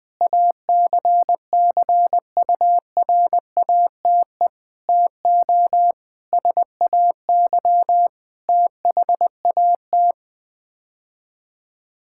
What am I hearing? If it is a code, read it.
Morse code, heard as a keyed tone, about 21 wpm, 700 Hz. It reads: ACCURATE TO SAY THAT
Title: Sets of 8 Words - 20wpm